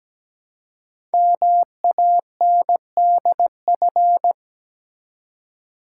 Morse code, heard as a keyed tone, about 17 wpm, 700 Hz. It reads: MANDF
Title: Code Groups - 5 Letters - 17wpm